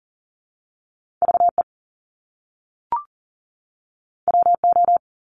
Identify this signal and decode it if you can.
Morse code, heard as a keyed tone, about 38 wpm, 700 Hz. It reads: VE WO
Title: Top Two Letter Combinations in Words 40wpm